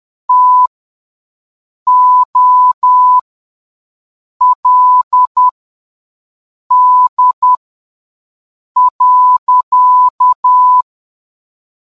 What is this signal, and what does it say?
Morse code, heard as a keyed tone, about 10 wpm, spaced as for 5 wpm, 1000 Hz. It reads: TOLD.